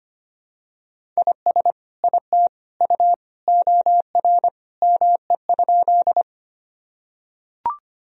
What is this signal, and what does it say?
Morse code, heard as a keyed tone, about 25 wpm, 700 Hz. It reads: IS IT U OR ME?
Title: Sentences from Top 200 Words and Abbreviations 25wpm